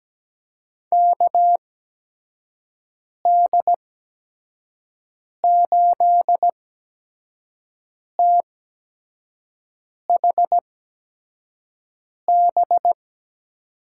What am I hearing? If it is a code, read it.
Morse code, heard as a keyed tone, about 17 wpm, spaced as for 5 wpm, 700 Hz. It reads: KD8THB
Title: USA Call Signs - 6 Characters - 8x Character Spacing - 17wpm